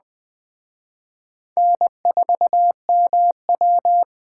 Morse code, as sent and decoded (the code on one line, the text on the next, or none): -. ....- -- .--
N4MW